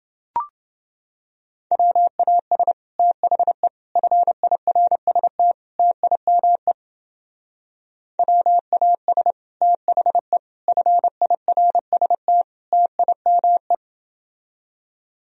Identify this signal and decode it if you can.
Morse code, heard as a keyed tone, about 31 wpm, 700 Hz. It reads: WAS THE FIRST TIME WAS THE FIRST TIME